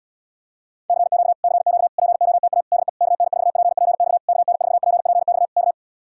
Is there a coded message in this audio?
Yes: Morse code from 0.9 to 5.7 s, heard as a keyed tone, about 67 wpm, 700 Hz. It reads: CQ CQ POTA DE KN4PPX KN4PPX K